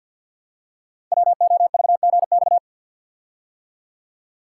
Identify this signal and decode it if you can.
Morse code, heard as a keyed tone, about 51 wpm, 700 Hz. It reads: WOVGX
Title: Code Groups - 5 Letters - 50wpm